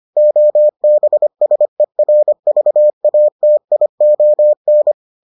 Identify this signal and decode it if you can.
Morse code, heard as a keyed tone, about 25 wpm, 600 Hz. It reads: OBSERVATION